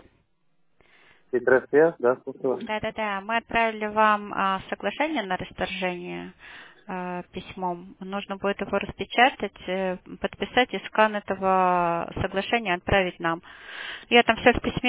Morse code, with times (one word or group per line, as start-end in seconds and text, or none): none